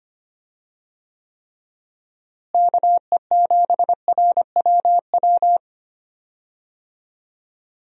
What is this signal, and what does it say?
Morse code, heard as a keyed tone, about 25 wpm, 700 Hz. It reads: KE7RWW
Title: USA Call Signs - 6 Characters - Repeated 2x - 25wpm